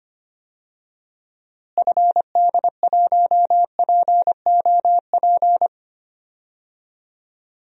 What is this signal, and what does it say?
Morse code, heard as a keyed tone, about 25 wpm, 700 Hz. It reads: FD1POP
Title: DXCC-100 Call Signs - 6 Characters - Repeated 2x - 25wpm